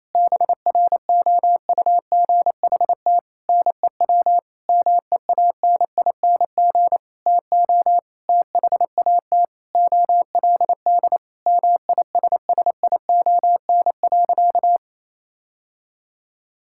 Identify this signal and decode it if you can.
Morse code, heard as a keyed tone, about 28 wpm, 700 Hz. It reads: BROUGHT NEW MEANING TO THAT OLD MISSION.
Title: Sentences from Top 1600 Words 28wpm